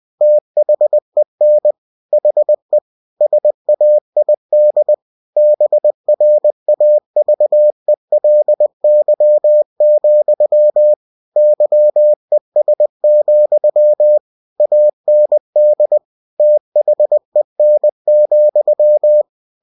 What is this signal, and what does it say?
Morse code, heard as a keyed tone, about 20 wpm, 600 Hz. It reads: THEN HE SAID BRAVELY, YES, AND THEN,
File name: cw-books / winnie-the-pooh / winnie-20wpm-page-0040